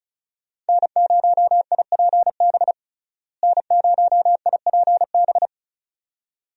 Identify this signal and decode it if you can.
Morse code, heard as a keyed tone, about 34 wpm, 700 Hz. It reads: N0IPB N0IPB
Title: USA Call Signs - 5 Characters - Repeated 2x - 35wpm